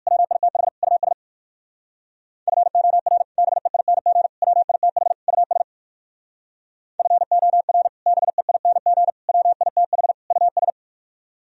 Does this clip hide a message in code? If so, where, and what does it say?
Yes: Morse code from 0.1 to 10.7 s, heard as a keyed tone, about 61 wpm, 700 Hz. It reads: WITH US FOR BEING WITH US FOR BEING WITH US